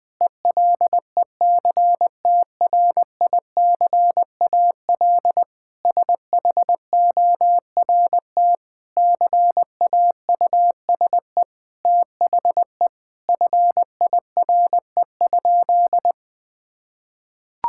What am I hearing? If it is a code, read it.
Morse code, heard as a keyed tone, about 20 wpm, 700 Hz. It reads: ELECTRICAL SHORT CAUSE THE FIRE?
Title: Sentences from Top 1000 Words - Review Part 6 - 20wpm